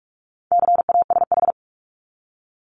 Morse code, heard as a keyed tone, about 44 wpm, 700 Hz. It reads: CASH
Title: Top 501-750 Words 45wpm